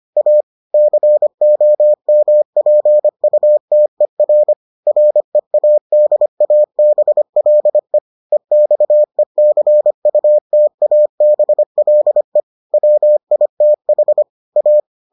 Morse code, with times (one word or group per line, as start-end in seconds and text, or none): 0.2-0.4 A
0.7-4.5 COMPUTER
4.9-8.0 READABLE
8.3-12.4 EXECUTABLE
12.7-14.2 WITH
14.6-14.8 A